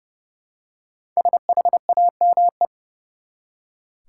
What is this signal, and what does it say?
Morse code, heard as a keyed tone, about 31 wpm, 700 Hz. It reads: SHAME
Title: Five Letter Words - 30wpm